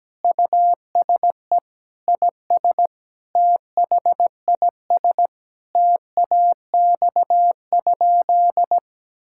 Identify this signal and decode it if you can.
Morse code, heard as a keyed tone, about 17 wpm, 700 Hz. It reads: USE IS THIS TAX?
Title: Sentences from Top 1000 Words - Review Part 5 - 17wpm